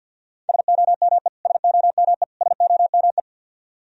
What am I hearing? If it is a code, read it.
Morse code, heard as a keyed tone, about 47 wpm, 700 Hz. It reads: SOME SOME SOME
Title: Top 100 Words Repeated 50wpm